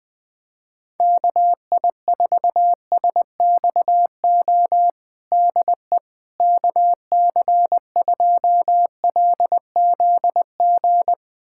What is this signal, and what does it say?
Morse code, heard as a keyed tone, about 20 wpm, 700 Hz. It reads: KI4SXO DE KC2LZG
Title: USA Callsign DE Callsign 20wpm